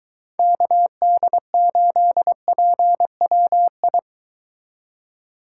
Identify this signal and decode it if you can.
Morse code, heard as a keyed tone, about 23 wpm, 700 Hz. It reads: KD8PWI